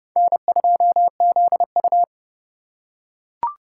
Morse code, heard as a keyed tone, about 29 wpm, 700 Hz. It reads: N2ZU